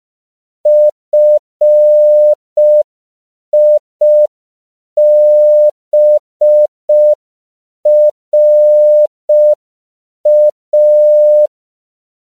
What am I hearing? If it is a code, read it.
Morse code, heard as a keyed tone, about 5 wpm, 600 Hz. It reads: FIBRA
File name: CW / 5PPM